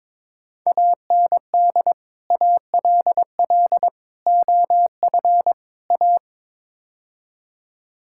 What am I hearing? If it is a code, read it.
Morse code, heard as a keyed tone, about 22 wpm, 700 Hz. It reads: AND ALL OF A
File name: practice / 22wpm / Build A Phrase - 5-Word Phrases - 22wpm